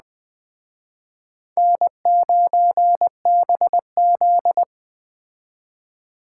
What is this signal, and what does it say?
Morse code, heard as a keyed tone, about 21 wpm, 700 Hz. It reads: N9BZ